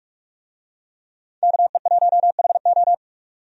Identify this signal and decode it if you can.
Morse code, heard as a keyed tone, about 46 wpm, 700 Hz. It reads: KE1HO
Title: USA Call Signs - 5 Characters - 45wpm